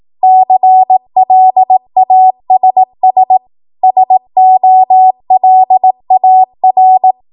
Morse code, heard as a keyed tone, about 18 wpm, 750 Hz. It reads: CLASS SOLAR
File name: Morse / 18WPM